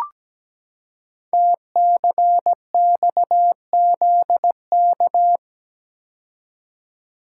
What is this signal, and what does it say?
Morse code, heard as a keyed tone, about 17 wpm, 700 Hz. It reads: TCXZK
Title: Code Groups - 5 Letters - 17wpm